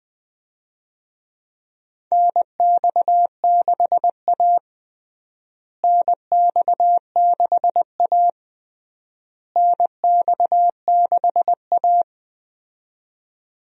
Morse code, heard as a keyed tone, about 20 wpm, 700 Hz. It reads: NX6A NX6A NX6A
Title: USA Call Signs - 4 Characters - Repeated 3x - 20wpm